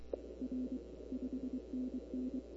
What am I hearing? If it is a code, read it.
Morse code, heard as a keyed tone, about 24 wpm, 260 Hz. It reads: R 5NN
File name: CW / 2005-k5tr arrl cw dx -23:00